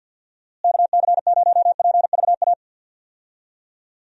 Morse code, heard as a keyed tone, about 52 wpm, 700 Hz. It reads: KC0PVA
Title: USA Call Signs - 6 Characters - Repeated 2x - 50wpm